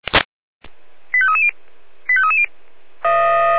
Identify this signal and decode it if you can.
Five-tone selective call (ZVEI), heard as quick runs of five tones, tones of about 70 ms. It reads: 85330 85330
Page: /496/SDR/ZVEI/85340/2022.05.11/